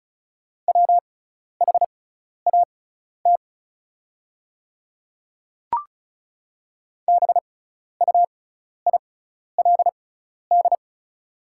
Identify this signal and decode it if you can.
Morse code, heard as a keyed tone, about 34 wpm, spaced as for 12 wpm, 690 Hz. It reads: WHAT BUILD
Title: 6x Character Spacing - Top 500 Words - 35wpm